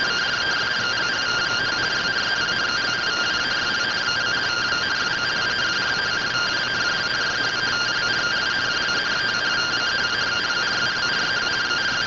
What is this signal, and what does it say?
Two-tone teleprinter signal, about 170 Hz shift.
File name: findings / CORONAL fr69 nb202 3mm.gen-17 rtty rx